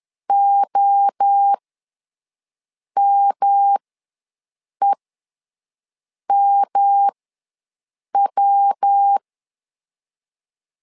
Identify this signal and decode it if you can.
Morse code, heard as a keyed tone, about 10 wpm, spaced as for 5 wpm, 800 Hz. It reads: OMEMW
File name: LEZIONI / 3 MWRO / 3 MWRO ws 10wpm 2310